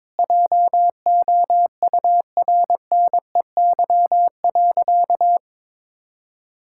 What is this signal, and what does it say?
Morse code, heard as a keyed tone, about 22 wpm, 700 Hz. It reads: JOURNEY.
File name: practice / 22wpm / Sentences with 2-Word Phrases - 22wpm